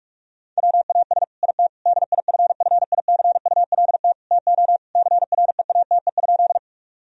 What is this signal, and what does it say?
Morse code, heard as a keyed tone, about 42 wpm, 700 Hz. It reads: WAS IT DIFFICULT TO CREATE?
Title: Sentences from Top 1000 Words - Review Part 3 - 45wpm